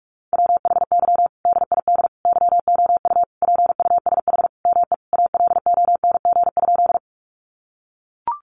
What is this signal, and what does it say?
Morse code, heard as a keyed tone, about 44 wpm, 700 Hz. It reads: WHY DID YOU PUSH ME ALONG?
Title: Sentences from Top 1300 Words 45wpm